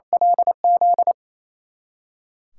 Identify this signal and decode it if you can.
Morse code, heard as a keyed tone, about 28 wpm, 700 Hz. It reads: LZ
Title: USA Callsign DE Callsign - 6 Characters - 28wpm